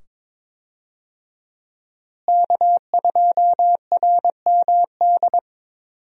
Morse code, heard as a keyed tone, about 22 wpm, 700 Hz. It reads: K2RMD
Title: USA Call Signs - 5 Characters - Repeated 2x - 22wpm